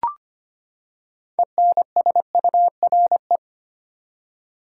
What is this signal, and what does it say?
Morse code, heard as a keyed tone, about 25 wpm, 700 Hz. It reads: ENSURE
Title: Six Letter Words - 25wpm